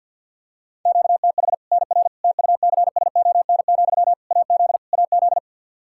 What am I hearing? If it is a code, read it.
Morse code, heard as a keyed tone, about 51 wpm, 700 Hz. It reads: QTH NR TUCSON, AZ AZ
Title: Practice QSO - Exchange 1 - Beginning - 50wpm